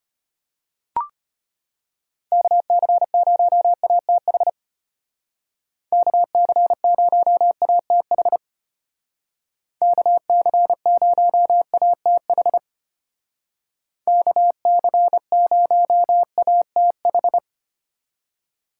Morse code, heard as a keyed tone, about 32 wpm, 700 Hz. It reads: KC0ATH KC0ATH KC0ATH KC0ATH